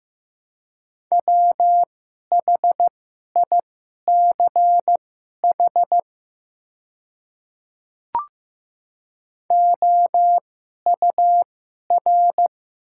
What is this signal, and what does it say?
Morse code, heard as a keyed tone, about 15 wpm, spaced as for 11 wpm, 700 Hz. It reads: WHICH OUR
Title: 2x Character Spacing - Top 200 Words - 15wpm